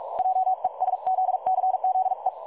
Morse code, heard as a keyed tone, about 42 wpm, 750 Hz. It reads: JESZCZE
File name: dlazaawansowanych / nagraniazpasma / websdr recording 3565.0kHz